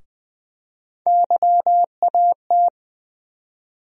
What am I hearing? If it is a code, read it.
Morse code, heard as a keyed tone, about 20 wpm, 700 Hz. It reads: YAT
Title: USA Call Sign Postfixes - 3 Letters 20wpm